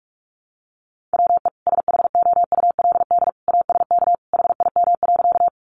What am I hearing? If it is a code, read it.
Morse code, heard as a keyed tone, about 44 wpm, 700 Hz. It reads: WE SHOULD ASK HIM.
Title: Sentences from Top 300 Words 45wpm